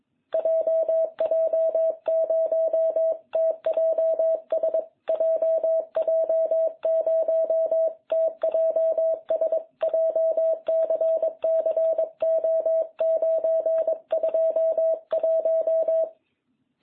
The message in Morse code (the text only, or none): JJ0TJS JJ0TJS JCC O921